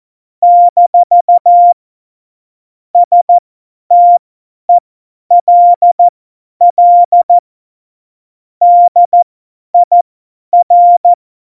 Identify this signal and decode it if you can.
Morse code, heard as a keyed tone, about 14 wpm, spaced as for 10 wpm, 700 Hz. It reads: - STELL DIR
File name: Morsekurs / Witze / Witz 05 Mai 2020, 50 F70